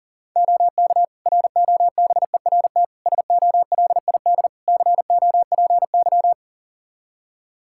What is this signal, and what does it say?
Morse code, heard as a keyed tone, about 40 wpm, 700 Hz. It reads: OK ROBERT SOLID COPY
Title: K1USN SST QSO Practice 40wpm